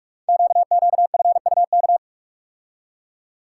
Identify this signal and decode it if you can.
Morse code, heard as a keyed tone, about 44 wpm, 700 Hz. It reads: QQFUK
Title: Code Groups - 5 Letters - 45wpm